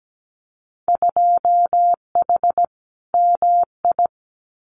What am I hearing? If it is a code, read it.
Morse code, heard as a keyed tone, about 17 wpm, 700 Hz. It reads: 2H MI
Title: Winter Field Day Exchanges 17wpm